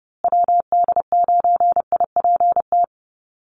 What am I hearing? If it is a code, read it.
Morse code, heard as a keyed tone, about 30 wpm, 700 Hz. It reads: WD9IPT
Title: USA Call Signs Repeated 3x 30wpm